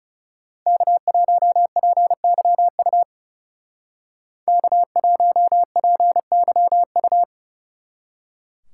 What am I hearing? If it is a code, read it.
Morse code, heard as a keyed tone, about 35 wpm, 700 Hz. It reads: K1PYU K1PYU
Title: Speed Racing - USA 1x3 Call Signs 30wpm